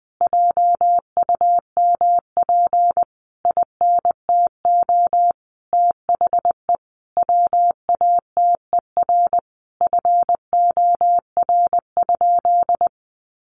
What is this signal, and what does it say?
Morse code, heard as a keyed tone, about 20 wpm, 700 Hz. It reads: JUMP INTO THE WATER FOR?
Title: Sentences from Top 1300 Words 20wpm